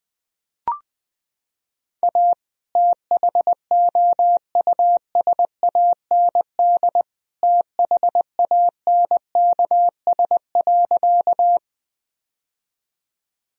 Morse code, heard as a keyed tone, about 21 wpm, 700 Hz. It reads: A THOUSAND THANKS.
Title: Sentences from Top 1000 Words - Review Part 3 - 20wpm